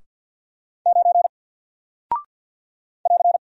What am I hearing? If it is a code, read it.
Morse code, heard as a keyed tone, about 48 wpm, 700 Hz. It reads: 9 <AR>
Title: Single Character (Letter:Number:Punctuation:Prosign) Alt 50wpm